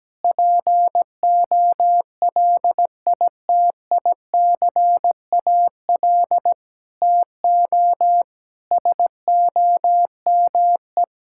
Morse code, heard as a keyed tone, about 17 wpm, 700 Hz. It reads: POLITICAL TO SOME
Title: Sentences from Top 1000 Words - Review Part 5 - 17wpm